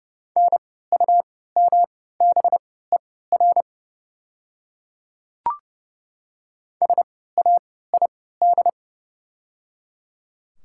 Morse code, heard as a keyed tone, about 30 wpm, spaced as for 17 wpm, 700 Hz. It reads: NUMBER SAID
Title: 3x Character Spacing - Top 100 Words - 30wpm